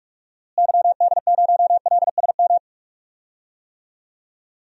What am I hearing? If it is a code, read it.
Morse code, heard as a keyed tone, about 41 wpm, 700 Hz. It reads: YD0LSM
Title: DXCC-100 Call Signs - 6 Characters - Repeated 3x - 45wpm